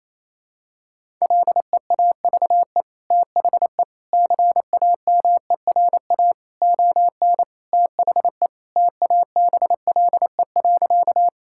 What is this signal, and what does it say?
Morse code, heard as a keyed tone, about 27 wpm, 700 Hz. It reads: LEAVE THE CAMERA ON THE TABLE.
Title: Sentences from Top 1000 Words - Review Part 5 - 28wpm